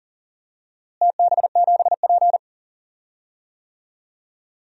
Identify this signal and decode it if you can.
Morse code, heard as a keyed tone, about 40 wpm, 700 Hz. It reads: TB7P
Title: DXCC-100 Call Signs - 4 Characters - Repeated 3x - 40wpm